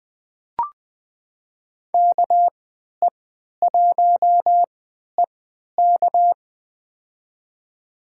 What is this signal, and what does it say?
Morse code, heard as a keyed tone, about 20 wpm, spaced as for 11 wpm, 700 Hz. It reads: KE1EK
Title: USA Call Signs - 5 Characters - 3x Character Spacing - 20wpm